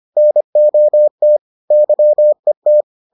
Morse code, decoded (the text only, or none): NOT YET